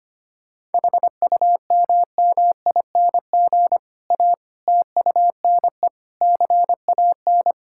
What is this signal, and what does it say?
Morse code, heard as a keyed tone, about 25 wpm, 700 Hz. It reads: HUMMING A TUNE CAN
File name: practice / 25wpm / Difficult Sentences with Hints - 25wpm